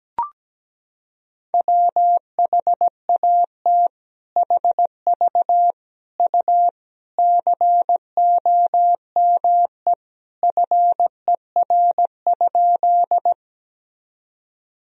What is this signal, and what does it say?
Morse code, heard as a keyed tone, about 17 wpm, 700 Hz. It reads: WHAT HV U COME FER?
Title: Sentences from Top 100 Words and Abbreviations 17wpm